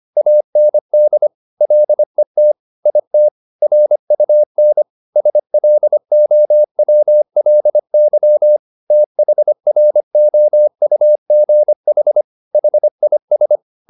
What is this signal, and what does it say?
Morse code, heard as a keyed tone, about 25 wpm, 600 Hz. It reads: AND LET IT RUN SLOWLY THROUGH HIS